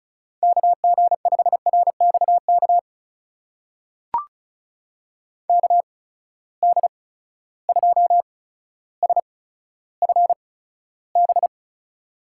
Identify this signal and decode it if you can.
Morse code, heard as a keyed tone, about 36 wpm, spaced as for 15 wpm, 700 Hz. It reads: KG5RXK KD2SFB
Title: USA Call Signs - 6 Characters - 8x Character Spacing - 35wpm